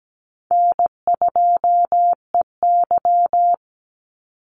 Morse code, heard as a keyed tone, about 17 wpm, 700 Hz. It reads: N2EY
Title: Call Signs Repeated 3x - 17wpm